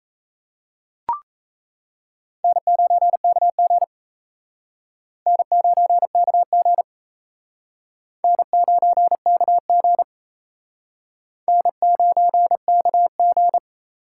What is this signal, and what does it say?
Morse code, heard as a keyed tone, about 34 wpm, 700 Hz. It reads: N9KG N9KG N9KG N9KG
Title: Speed Racing - USA Call Signs 28wpm